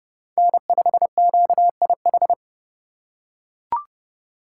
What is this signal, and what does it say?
Morse code, heard as a keyed tone, about 32 wpm, 700 Hz. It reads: N5QIH
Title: USA Call Signs - 5 Characters - 2x Character Spacing - 30wpm